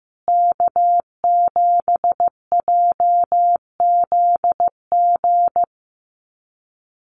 Morse code, heard as a keyed tone, about 15 wpm, 700 Hz. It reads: K7JZG
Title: USA Call Signs Repeated 3x 15wpm